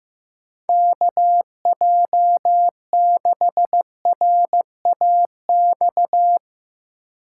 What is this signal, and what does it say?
Morse code, heard as a keyed tone, about 15 wpm, 700 Hz. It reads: KJ6RAX
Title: USA Call Signs - 6 Characters - 6x Character Spacing - 15wpm